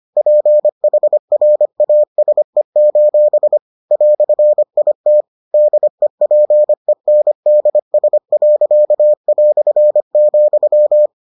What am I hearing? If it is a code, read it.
Morse code, heard as a keyed tone, about 25 wpm, 600 Hz. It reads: PHRASE: "IT DEPENDS.",